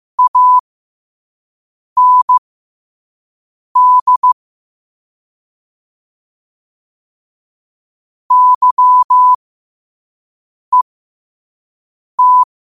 Morse code, heard as a keyed tone, about 15 wpm, spaced as for 5 wpm, 1000 Hz. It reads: AND YET
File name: qotdfiles / 5wpm